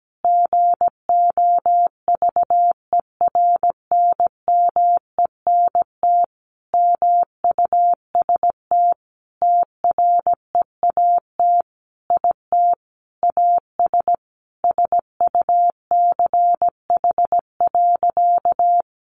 Morse code, heard as a keyed tone, about 17 wpm, 700 Hz. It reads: GOVERNMENT MUST TREAT IT AS SUCH.
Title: Sentences from Top 1300 Words 17wpm